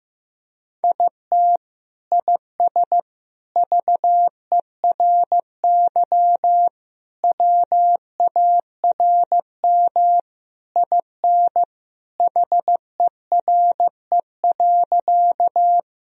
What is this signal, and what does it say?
Morse code, heard as a keyed tone, about 16 wpm, 700 Hz. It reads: IT IS VERY WARM IN HERE.
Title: Sentences from Top 1000 Words - Review Part 2 - 15wpm